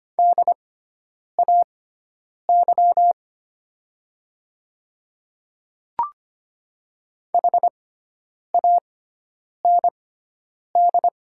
Morse code, heard as a keyed tone, about 25 wpm, spaced as for 9 wpm, 700 Hz. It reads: DAY HAND
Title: 6x Character Spacing - Top 200 Words - 25wpm